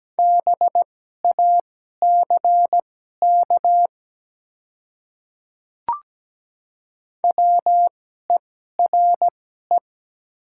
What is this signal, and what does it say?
Morse code, heard as a keyed tone, about 17 wpm, spaced as for 12 wpm, 700 Hz. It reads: BACK WERE
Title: 2x Character Spacing - Top 200 Words - 17wpm